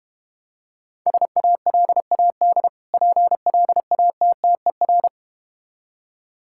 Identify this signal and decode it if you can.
Morse code, heard as a keyed tone, about 33 wpm, 700 Hz. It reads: SALAD PLATTER